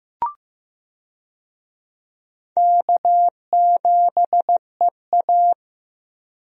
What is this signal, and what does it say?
Morse code, heard as a keyed tone, about 15 wpm, 700 Hz. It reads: K7EA
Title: USA Call Signs - 4 Characters - Repeated 3x - 15wpm